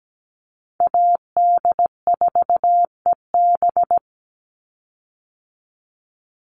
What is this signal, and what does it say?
Morse code, heard as a keyed tone, about 17 wpm, 700 Hz. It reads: AD4EB
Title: Call Signs - 17wpm